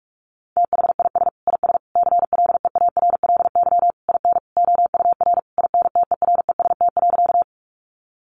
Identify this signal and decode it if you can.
Morse code, heard as a keyed tone, about 44 wpm, 700 Hz. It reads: THIS IS CLEARLY IN OUR INTEREST.
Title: Sentences from Top 1300 Words 45wpm